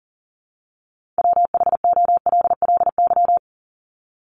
Morse code, heard as a keyed tone, about 40 wpm, 700 Hz. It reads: WHOLLY
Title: LY Suffix - Top 100 Words with Base Words 40wpm